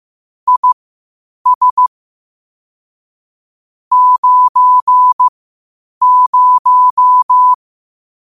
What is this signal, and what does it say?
Morse code, heard as a keyed tone, about 15 wpm, spaced as for 8 wpm, 1000 Hz. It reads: IS 90
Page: Quotes for Thu, 14 Aug 2025 in Morse Code at 8 words per minute.